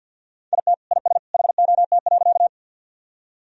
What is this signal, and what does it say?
Morse code, heard as a keyed tone, about 45 wpm, 700 Hz. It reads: IT IS HOT.
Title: Sentences from Top 1000 Words - Review Part 3 - 50wpm